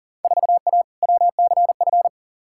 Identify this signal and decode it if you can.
Morse code, heard as a keyed tone, about 40 wpm, 700 Hz. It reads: 4A WCF
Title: Field Day Exchanges - 40wpm